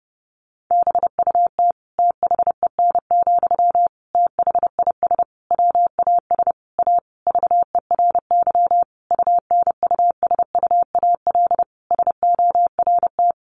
Morse code, heard as a keyed tone, about 30 wpm, 700 Hz. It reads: BUT THEN, THIS WAS A VERY UNUSUAL SORT
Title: Sentences from Top 1400 Words 30wpm